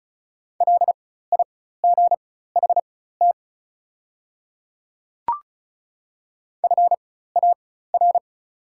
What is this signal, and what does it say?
Morse code, heard as a keyed tone, about 36 wpm, spaced as for 16 wpm, 690 Hz. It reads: LIGHT FAR
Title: 4x Character Spacing - Top 500 Words - 35wpm